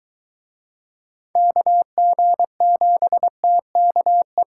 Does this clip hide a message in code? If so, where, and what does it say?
Yes: Morse code from 1.4 to 4.4 s, heard as a keyed tone, about 23 wpm, 700 Hz. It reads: KG7TKE